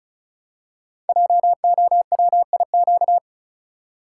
Morse code, heard as a keyed tone, about 36 wpm, 690 Hz. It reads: JOWIQ
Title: Code Groups - 5 Letters - 35wpm